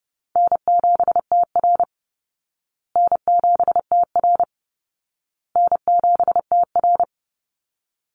Morse code, heard as a keyed tone, about 30 wpm, 700 Hz. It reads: N7TR N7TR N7TR
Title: Call Signs Repeated 2x - 30wpm